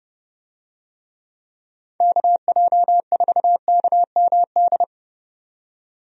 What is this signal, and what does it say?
Morse code, heard as a keyed tone, about 30 wpm, 700 Hz. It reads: KJ4KMD
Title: USA Call Signs - 6 Characters - 30wpm